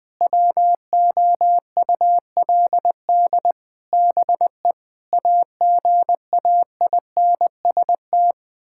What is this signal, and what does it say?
Morse code, heard as a keyed tone, about 20 wpm, 700 Hz. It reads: WOULD BE AGAINST
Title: Sentences from Top 1000 Words - Review Part 6 - 20wpm